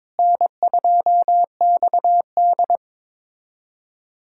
Morse code, heard as a keyed tone, about 21 wpm, 700 Hz. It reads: N2XD
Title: USA Callsign DE Callsign - 4 Characters - 22wpm